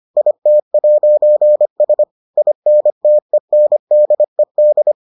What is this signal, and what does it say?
Morse code, heard as a keyed tone, about 25 wpm, 600 Hz. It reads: IT'S INTENDED